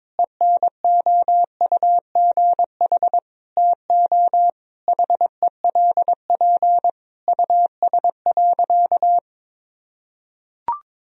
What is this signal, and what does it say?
Morse code, heard as a keyed tone, about 22 wpm, 700 Hz. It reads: ENOUGH TO HELP US.
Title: Sentences from Top 1000 Words - Review Part 3 - 22wpm